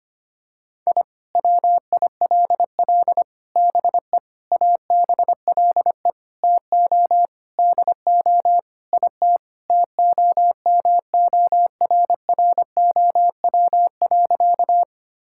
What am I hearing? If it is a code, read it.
Morse code, heard as a keyed tone, about 25 wpm, 700 Hz. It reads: I WILL BE ABLE TO DO IT TOMORROW.
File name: practice / 25wpm / Sentences with 4-Word Phrases - 25wpm